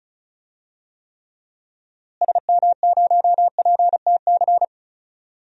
Morse code, heard as a keyed tone, about 35 wpm, 700 Hz. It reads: SM0PTC
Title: DXCC-100 Call Signs - 6 Characters - Repeated 3x - 35wpm